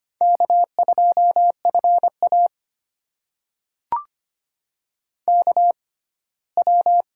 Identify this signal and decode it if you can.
Morse code, heard as a keyed tone, about 25 wpm, spaced as for 17 wpm, 700 Hz. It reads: K2FA KW
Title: USA Call Signs - 6x Character Spacing - 25wpm